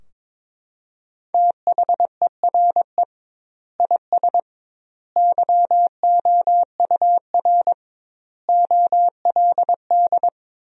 Morse code, heard as a keyed tone, about 21 wpm, 700 Hz. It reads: THERE IS YOUR OLD
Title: 2x Word-Spacing - Sentences from Top 500 Words - 22wpm